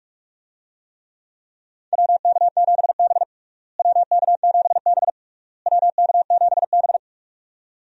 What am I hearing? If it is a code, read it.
Morse code, heard as a keyed tone, about 46 wpm, 700 Hz. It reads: WK7B WK7B WK7B
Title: USA Call Signs - 4 Characters - Repeated 3x - 45wpm